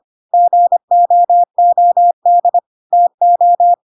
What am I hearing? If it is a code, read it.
Morse code, heard as a keyed tone, about 25 wpm, 700 Hz. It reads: GOOD TO
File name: Pinocchio / Pinocchio ch5 25 25 ik1hge